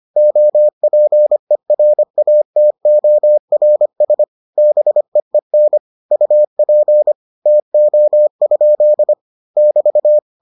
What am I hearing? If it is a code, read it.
Morse code, heard as a keyed tone, about 25 wpm, 600 Hz. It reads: OPERATORS BEEN UP TO? <BT>